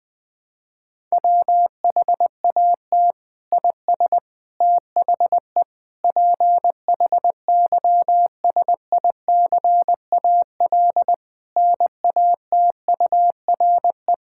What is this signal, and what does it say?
Morse code, heard as a keyed tone, about 20 wpm, 700 Hz. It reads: WHAT IS THE PHYSICAL NATURE
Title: Sentences from Top 1000 Words - Review Part 6 - 20wpm